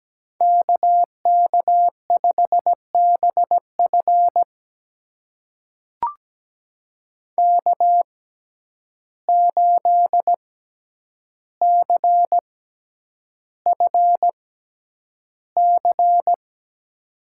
Morse code, heard as a keyed tone, about 17 wpm, spaced as for 9 wpm, 700 Hz. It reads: KK5BF K8CFC
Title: USA Call Signs - 6x Character Spacing - 17wpm